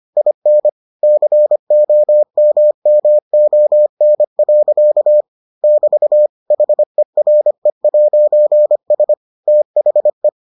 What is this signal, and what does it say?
Morse code, heard as a keyed tone, about 25 wpm, 600 Hz. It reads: IN COMMON. <BT> HERE'S THE